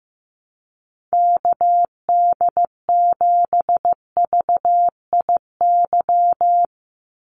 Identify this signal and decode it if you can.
Morse code, heard as a keyed tone, about 15 wpm, 700 Hz. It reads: KD7VIY
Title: USA Call Signs 15wpm